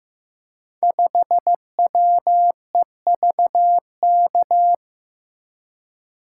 Morse code, heard as a keyed tone, about 15 wpm, 700 Hz. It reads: 5WEVK
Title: Code Groups - 5 Letter:Numbers - 15wpm